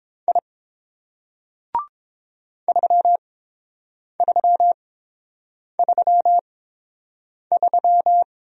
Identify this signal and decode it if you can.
Morse code, heard as a keyed tone, about 30 wpm, spaced as for 7 wpm, 690 Hz. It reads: I 3333